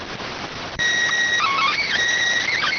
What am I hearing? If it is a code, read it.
SSTV (slow-scan television), PD 290.